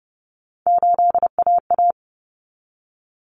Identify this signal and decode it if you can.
Morse code, heard as a keyed tone, about 30 wpm, 700 Hz. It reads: 8AA
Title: Common Cut Numbers - 3-Digit 30wpm